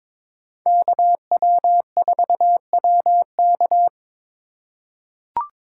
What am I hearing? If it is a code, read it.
Morse code, heard as a keyed tone, about 22 wpm, 700 Hz. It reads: KW4WK
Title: USA Call Signs - 5 Characters - 2x Character Spacing - 22wpm